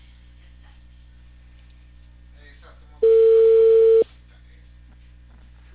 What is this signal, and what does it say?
Ringback tone (China, 450 Hz), with speech.